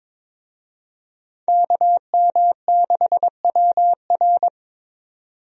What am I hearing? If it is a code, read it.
Morse code, heard as a keyed tone, about 22 wpm, 700 Hz. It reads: KM6WR
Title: USA Call Signs - 5 Characters - 22wpm